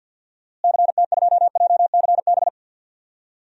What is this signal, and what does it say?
Morse code, heard as a keyed tone, about 50 wpm, 700 Hz. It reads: CT2JCB
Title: DXCC-100 Call Signs - 6 Characters - Repeated 2x - 50wpm